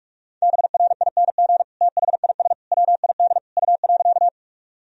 Morse code, heard as a keyed tone, about 46 wpm, 690 Hz. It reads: BRING THIS WID U.
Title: Sentences from Top 500 Words and Abbreviations 45wpm